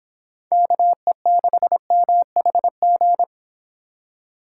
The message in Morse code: KE6MHG

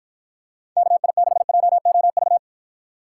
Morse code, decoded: KI6JOV

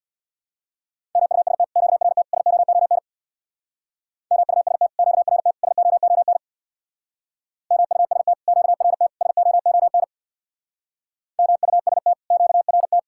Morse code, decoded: MUST QRT SOON MUST QRT SOON MUST QRT SOON MUST QRT

